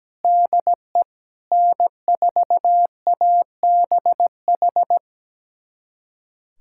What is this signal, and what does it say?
Morse code, heard as a keyed tone, about 17 wpm, 700 Hz. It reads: DE N4ABH
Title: USA Callsign DE Callsign - 5 Characters - 17wpm